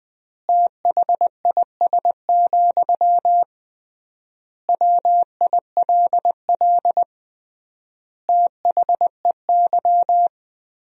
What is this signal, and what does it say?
Morse code, heard as a keyed tone, about 21 wpm, 700 Hz. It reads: THIS, WILL THEY
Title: 3x Word-Spacing - Sentences from Top 400 Words - 20wpm